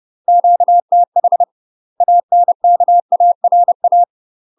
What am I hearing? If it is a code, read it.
Morse code, heard as a keyed tone, about 30 wpm, 700 Hz. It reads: QTH ANKARA
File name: UK MASTER QSOs TRAINING / 30WPM / UK MASTER QSOs 0008